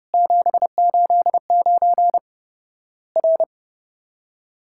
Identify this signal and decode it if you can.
Morse code, heard as a keyed tone, about 30 wpm, 700 Hz. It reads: 789 R
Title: Worked All Europe DX Contest - QTC Exchange - 30wpm